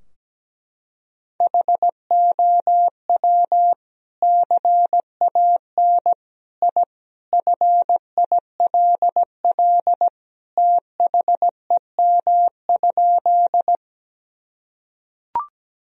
Morse code, heard as a keyed tone, about 17 wpm, 700 Hz. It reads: HOW CAN I FILL THEM?
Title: Sentences from Top 1000 Words - Review Part 3 - 17wpm